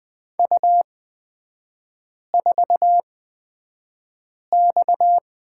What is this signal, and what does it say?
Morse code, heard as a keyed tone, about 20 wpm, spaced as for 5 wpm, 690 Hz. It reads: U4X
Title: 229 Lesson 37e - Letter x - Three Characters 20wpm